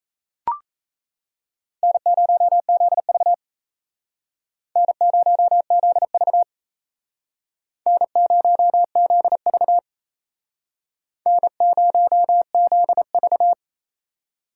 Morse code, heard as a keyed tone, about 35 wpm, 700 Hz. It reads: N0ZV N0ZV N0ZV N0ZV